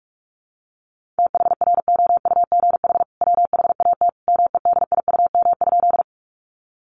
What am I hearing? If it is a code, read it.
Morse code, heard as a keyed tone, about 44 wpm, 700 Hz. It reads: THROUGH WHAT MEDIUM?
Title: Sentences from Top 900 Words 45wpm